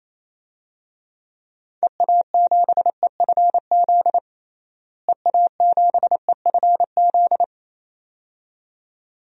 Morse code, heard as a keyed tone, about 28 wpm, 700 Hz. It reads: EA7EFZ EA7EFZ
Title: DXCC-100 Call Signs - 6 Characters - Repeated 2x - 28wpm